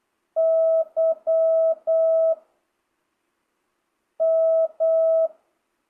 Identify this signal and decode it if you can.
Morse code, heard as a keyed tone, about 8 wpm, 660 Hz. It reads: Y M
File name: jh8cbh / 2025cw